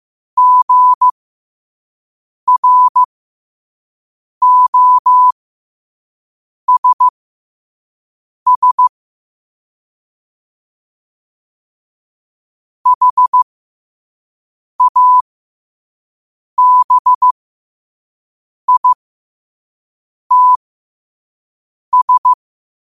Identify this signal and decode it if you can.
Morse code, heard as a keyed tone, about 15 wpm, spaced as for 5 wpm, 1000 Hz. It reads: GROSS HABITS
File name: qotdfiles / 5wpm